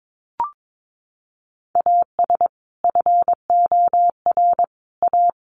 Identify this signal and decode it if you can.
Morse code, heard as a keyed tone, about 22 wpm, 700 Hz. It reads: AS FOR A
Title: Sentences from Top 1400 Words 22wpm